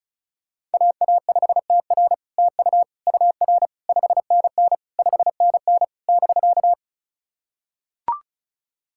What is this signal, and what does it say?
Morse code, heard as a keyed tone, about 36 wpm, 700 Hz. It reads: AA5TR TU UR 5NN 5NN <BK>
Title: SOTA QSO Practice 35wpm